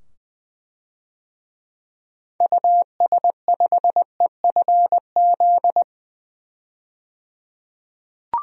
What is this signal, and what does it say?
Morse code, heard as a keyed tone, about 20 wpm, 700 Hz. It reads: US5EFZ
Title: DXCC-100 Call Signs - 6 Characters - 20wpm